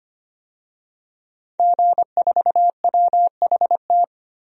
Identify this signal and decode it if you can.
Morse code, heard as a keyed tone, about 25 wpm, 700 Hz. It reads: G4WHT
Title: DXCC-100 Call Signs - 5 Characters - 25wpm